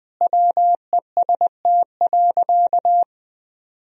Morse code, heard as a keyed tone, about 20 wpm, 700 Hz. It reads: WEST.
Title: Sentences from Top 1000 Words - Review Part 2 - 20wpm